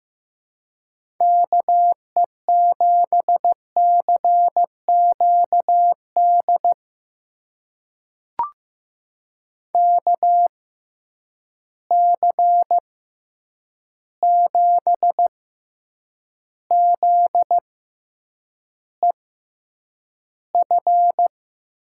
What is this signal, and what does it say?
Morse code, heard as a keyed tone, about 15 wpm, spaced as for 8 wpm, 700 Hz. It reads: KE7CQD KC7ZEF
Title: USA Call Signs - 6 Characters - 6x Character Spacing - 15wpm